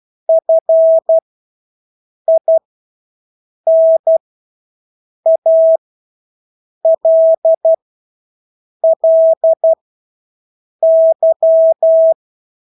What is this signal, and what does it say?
Morse code, heard as a keyed tone, about 12 wpm, spaced as for 6 wpm, 650 Hz. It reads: FINALLY
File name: Engelske ord 12-6 wpm